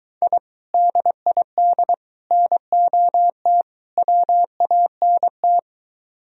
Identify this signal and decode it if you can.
Morse code, heard as a keyed tone, about 23 wpm, 700 Hz. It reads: I DID NOT WANT